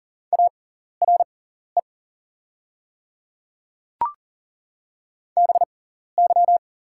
Morse code, heard as a keyed tone, about 40 wpm, spaced as for 14 wpm, 700 Hz. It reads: ARE BY
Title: 6x Character Spacing - Top 300 Words - 40wpm